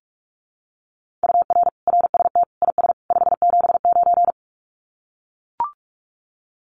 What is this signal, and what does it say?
Morse code, heard as a keyed tone, about 44 wpm, 700 Hz. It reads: UR RST IS 579
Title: Common QSO Phrases 45wpm